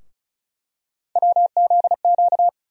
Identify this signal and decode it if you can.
Morse code, heard as a keyed tone, about 36 wpm, 700 Hz. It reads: WZQ
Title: USA Call Sign Postfixes - 3 Letters 35wpm